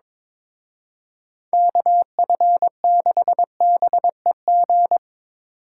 Morse code, heard as a keyed tone, about 25 wpm, 700 Hz. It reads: KF6BEG
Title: Speed Racing - USA 2x3 Call Signs 22wpm